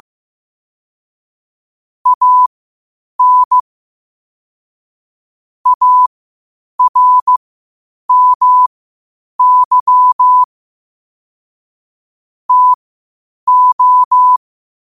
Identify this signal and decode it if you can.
Morse code, heard as a keyed tone, about 15 wpm, spaced as for 8 wpm, 1000 Hz. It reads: AN ARMY TO